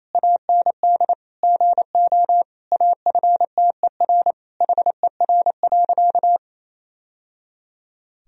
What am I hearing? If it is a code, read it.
Morse code, heard as a keyed tone, about 28 wpm, 700 Hz. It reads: AND GO AFTER HER.
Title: Sentences from Top 1000 Words - Review Part 3 - 28wpm